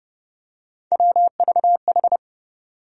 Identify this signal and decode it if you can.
Morse code, heard as a keyed tone, about 30 wpm, 700 Hz. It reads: WVH
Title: USA Call Sign Postfixes - 3 Letters 30wpm